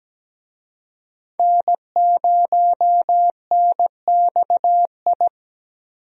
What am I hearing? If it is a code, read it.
Morse code, heard as a keyed tone, about 17 wpm, 700 Hz. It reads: N0NXI
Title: USA Call Signs - 5 Characters - Repeated 3x - 17wpm